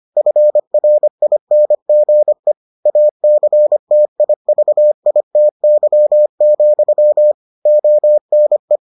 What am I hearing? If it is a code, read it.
Morse code, heard as a keyed tone, about 25 wpm, 600 Hz. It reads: FRINGE ACTIVITY, ONE